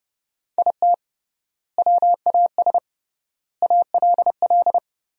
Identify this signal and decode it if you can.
Morse code, heard as a keyed tone, about 30 wpm, 700 Hz. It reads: IT WAS ALL
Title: 3x Word-Spacing - Sentences from Top 400 Words - 30wpm